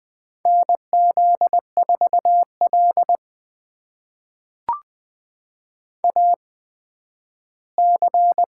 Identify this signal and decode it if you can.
Morse code, heard as a keyed tone, about 20 wpm, spaced as for 7 wpm, 700 Hz. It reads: NZ4LAC
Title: USA Call Signs - 8x Character Spacing - 20wpm